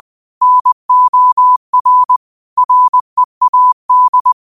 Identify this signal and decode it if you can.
Morse code, heard as a keyed tone, about 20 wpm, 1000 Hz. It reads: NOR READ